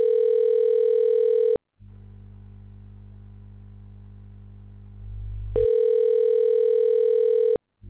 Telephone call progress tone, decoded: ringback tone

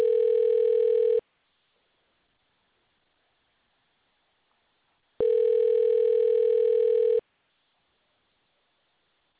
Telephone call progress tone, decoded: ringback tone

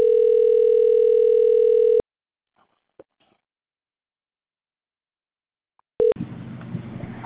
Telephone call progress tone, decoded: ringback tone

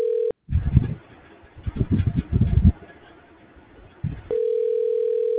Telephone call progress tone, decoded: ringback tone

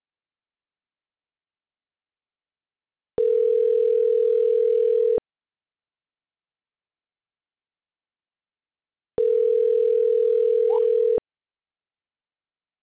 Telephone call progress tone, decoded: ringback tone

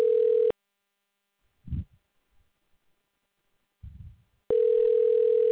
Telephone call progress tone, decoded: ringback tone

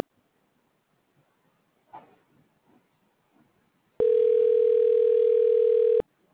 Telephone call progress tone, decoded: ringback tone

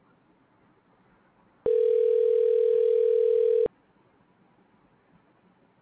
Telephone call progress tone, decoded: ringback tone

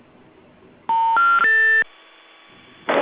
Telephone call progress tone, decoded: special information tone

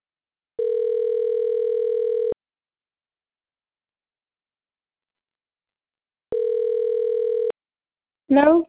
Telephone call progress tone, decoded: ringback tone

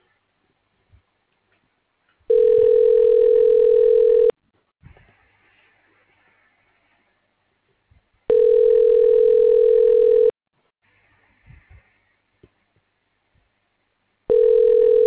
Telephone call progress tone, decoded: ringback tone

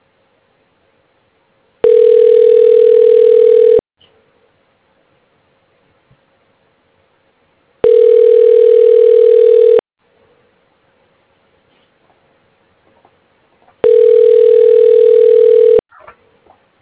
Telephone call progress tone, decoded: ringback tone

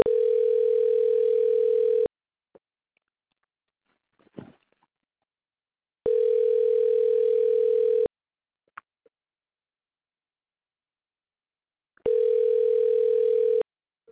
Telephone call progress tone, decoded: ringback tone

